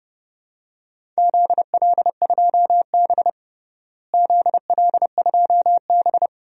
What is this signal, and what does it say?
Morse code, heard as a keyed tone, about 30 wpm, 700 Hz. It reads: ZL2B ZL2B
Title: DXCC-100 Call Signs - 4 Characters - Repeated 2x - 30wpm